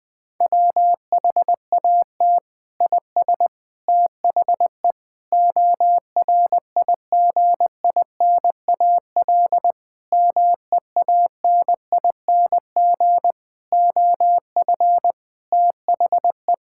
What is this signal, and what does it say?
Morse code, heard as a keyed tone, about 21 wpm, 700 Hz. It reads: WHAT IS THE ORIGINAL MEANING OF THE
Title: Sentences from Top 1000 Words - Review Part 6 - 20wpm